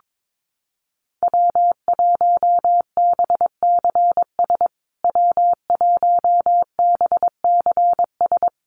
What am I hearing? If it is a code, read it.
Morse code, heard as a keyed tone, about 22 wpm, 700 Hz. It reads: W1BCS W1BCS
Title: USA Call Signs Repeated 3x 22wpm